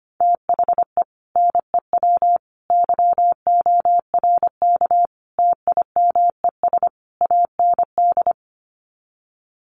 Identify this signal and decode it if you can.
Morse code, heard as a keyed tone, about 25 wpm, 700 Hz. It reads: THE NEW YORK TIMES AND
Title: Sets of 5 Words Encore 25wpm